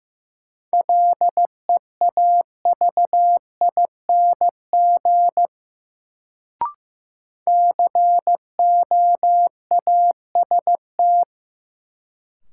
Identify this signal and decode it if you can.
Morse code, heard as a keyed tone, about 15 wpm, 700 Hz. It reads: LEAVING COAST
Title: Sentences from Top 1000 Words - Review Part 4 - New Word Drill - 15wpm